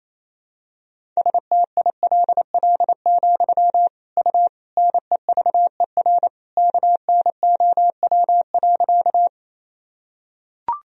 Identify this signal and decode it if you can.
Morse code, heard as a keyed tone, about 28 wpm, 700 Hz. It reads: STILL, U NEVER KNOW.
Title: Sentences from Top 300 Words and Abbreviations 28wpm